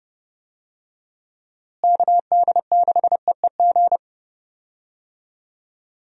Morse code, heard as a keyed tone, about 32 wpm, 700 Hz. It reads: KD6EEG
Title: USA Call Signs - 6 Characters - Repeated 2x - 30wpm